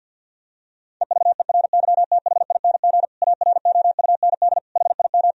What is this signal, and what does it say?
Morse code, heard as a keyed tone, about 50 wpm, 700 Hz. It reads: EVERYTHING AROUND HIM
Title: Sentences from Top 1000 Words - Review Part 3 - 50wpm